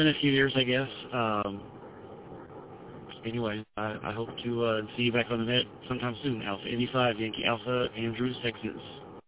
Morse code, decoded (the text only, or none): none